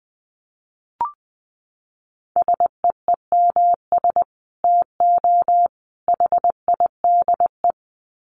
Morse code, heard as a keyed tone, about 20 wpm, 700 Hz. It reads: SEEMS TO HIDE
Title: Dit Challenge 20wpm